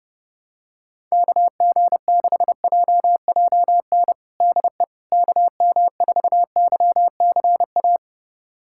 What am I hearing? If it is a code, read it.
Morse code, heard as a keyed tone, about 30 wpm, 700 Hz. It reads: KG6JJN DE KM4YCA
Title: USA Callsign DE Callsign - 6 Characters - 30wpm